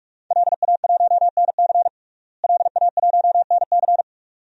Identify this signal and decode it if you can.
Morse code, heard as a keyed tone, about 42 wpm, 700 Hz. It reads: LA1NC LA1NC
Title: DXCC-100 Call Signs - 5 Characters - Repeated 2x - 45wpm